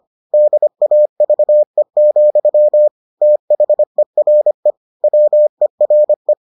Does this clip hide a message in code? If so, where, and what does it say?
Yes: Morse code from 0.3 to 6.3 s, heard as a keyed tone, about 25 wpm, 600 Hz. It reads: DAVE, THERE WERE